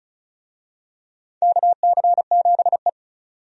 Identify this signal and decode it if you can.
Morse code, heard as a keyed tone, about 35 wpm, 700 Hz. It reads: KC7E
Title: USA Call Signs - 4 Characters - 35wpm